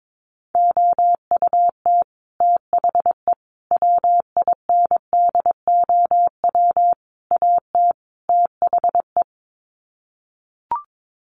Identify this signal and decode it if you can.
Morse code, heard as a keyed tone, about 22 wpm, 700 Hz. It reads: OUT THE WINDOW AT THE
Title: Sets of 5 Words Encore2 22wpm